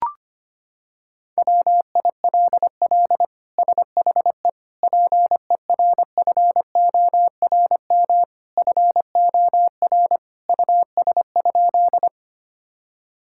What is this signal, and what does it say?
Morse code, heard as a keyed tone, about 25 wpm, 700 Hz. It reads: WILL SHE PERFORM FOR US?
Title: Sentences from Top 1000 Words - Review Part 5 - 25wpm